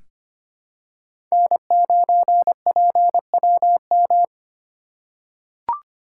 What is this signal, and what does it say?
Morse code, heard as a keyed tone, about 25 wpm, 700 Hz. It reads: N9PWM
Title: USA Call Signs - 2x Character Spacing - 25wpm